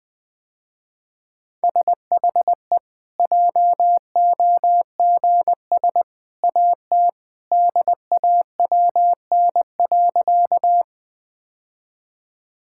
Morse code, heard as a keyed tone, about 20 wpm, 700 Hz. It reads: SHE JOGS AT DAWN.
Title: Easy Sentences with Hints - 20wpm